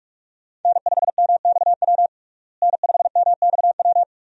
Morse code, heard as a keyed tone, about 42 wpm, 700 Hz. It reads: N5MXW N5MXW
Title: USA Call Signs - 5 Characters - Repeated 3x - 45wpm